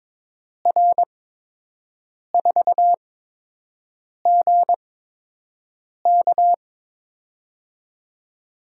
Morse code, heard as a keyed tone, about 22 wpm, spaced as for 6 wpm, 700 Hz. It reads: R4GK
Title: USA Call Signs - 5 Characters - 8x Character Spacing - 22wpm